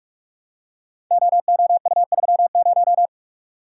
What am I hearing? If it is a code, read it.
Morse code, heard as a keyed tone, about 44 wpm, 700 Hz. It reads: OOU30
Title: Code Groups - 5 Letter:Numbers - 45wpm